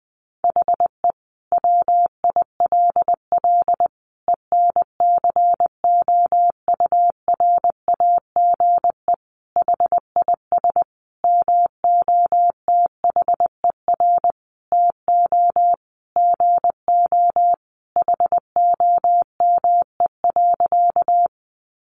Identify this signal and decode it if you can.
Morse code, heard as a keyed tone, about 20 wpm, 700 Hz. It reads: HE WILL ENCOURAGE HIS MOTHER TO GO HOME.
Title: Sentences from Top 1400 Words 20wpm